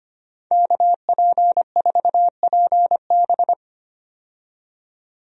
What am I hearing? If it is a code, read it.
Morse code, heard as a keyed tone, about 25 wpm, 700 Hz. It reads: KP4PB
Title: DXCC-100 Call Signs - 5 Characters - 25wpm